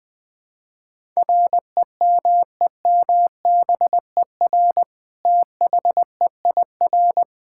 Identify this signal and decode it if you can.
Morse code, heard as a keyed tone, about 20 wpm, 700 Hz. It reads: REMEMBER THEIR
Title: Sentences from Top 1000 Words - Review Part 5 - 20wpm